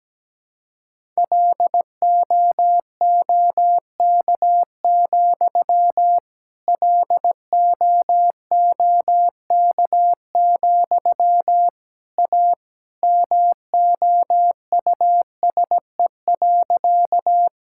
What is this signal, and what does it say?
Morse code, heard as a keyed tone, about 17 wpm, 700 Hz. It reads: LOOK, LOOK, A MOUSE.
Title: Sentences from Top 1000 Words - Review Part 5 - 17wpm